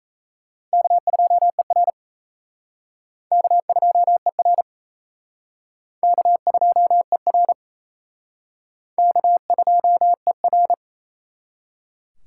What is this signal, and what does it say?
Morse code, heard as a keyed tone, about 37 wpm, 700 Hz. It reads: K2ER K2ER K2ER K2ER